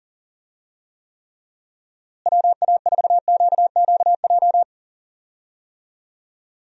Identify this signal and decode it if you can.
Morse code, heard as a keyed tone, about 37 wpm, 700 Hz. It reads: WA4QQJ
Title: USA Call Signs - 6 Characters - Repeated 2x - 40wpm